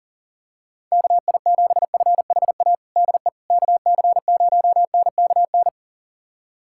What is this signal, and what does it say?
Morse code, heard as a keyed tone, about 40 wpm, 700 Hz. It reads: KI7FHA DE KC0NKN
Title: USA Callsign DE Callsign 40wpm